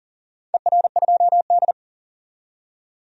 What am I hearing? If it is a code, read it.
Morse code, heard as a keyed tone, about 39 wpm, 700 Hz. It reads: ER2D
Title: DXCC-100 Call Signs - 4 Characters - 40wpm